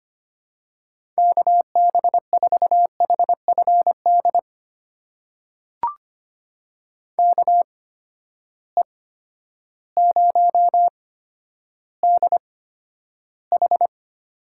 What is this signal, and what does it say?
Morse code, heard as a keyed tone, about 25 wpm, spaced as for 11 wpm, 700 Hz. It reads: KB4HFD KE0DH
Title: USA Call Signs - 6 Characters - 8x Character Spacing - 25wpm